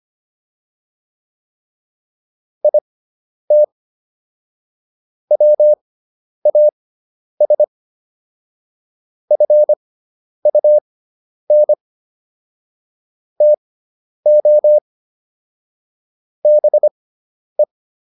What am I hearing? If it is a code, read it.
Morse code, heard as a keyed tone, about 25 wpm, spaced as for 10 wpm, 600 Hz. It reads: IT WAS FUN TO BE